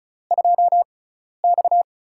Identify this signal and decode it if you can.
Morse code, heard as a keyed tone, about 36 wpm, spaced as for 12 wpm, 700 Hz. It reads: 2X
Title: USA Call Signs - 4 Characters - 6x Character Spacing - 35wpm